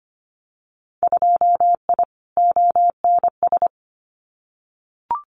Morse code, heard as a keyed tone, about 25 wpm, 700 Hz. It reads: 2I ONS
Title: Winter Field Day Exchanges Encore 25wpm